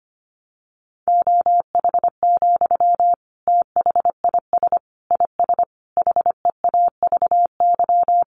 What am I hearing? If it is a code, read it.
Morse code, heard as a keyed tone, about 25 wpm, 700 Hz. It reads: OH, THIS IS HEAVY
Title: Sentences from Top 1100 Words 25wpm